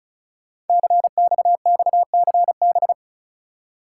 Morse code, heard as a keyed tone, about 36 wpm, 690 Hz. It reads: CXXCB
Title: Code Groups - 5 Letters - 35wpm